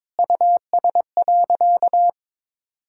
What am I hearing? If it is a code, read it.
Morse code, heard as a keyed tone, about 22 wpm, 700 Hz. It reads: US.
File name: practice / 22wpm / Sentences with 3-Word Phrases - 22wpm